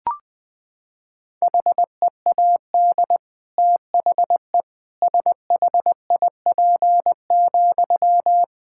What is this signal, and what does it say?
Morse code, heard as a keyed tone, about 20 wpm, 700 Hz. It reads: HEAD THE SHIP,
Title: Sentences from Top 1000 Words - Review Part 5 - 20wpm